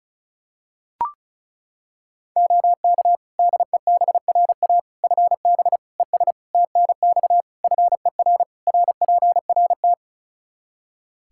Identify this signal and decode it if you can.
Morse code, heard as a keyed tone, about 34 wpm, 700 Hz. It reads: OK DEBRA FB ES TNX FER RPRT
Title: Practice QSO - Exchange 2 - Info - 35wpm